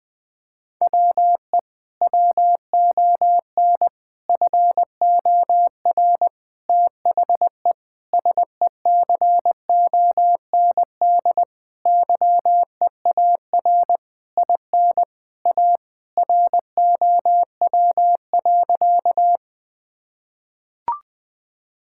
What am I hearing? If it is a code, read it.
Morse code, heard as a keyed tone, about 20 wpm, 700 Hz. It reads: WE WON FOR THE SECOND YEAR IN A ROW.
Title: Sentences with 6-Word Phrases - 20wpm